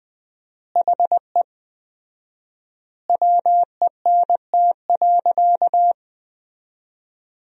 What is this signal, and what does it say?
Morse code, heard as a keyed tone, about 20 wpm, 700 Hz. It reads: HE WENT.
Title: 4x Word-Spacing - Sentences from Top 200 Words - 20wpm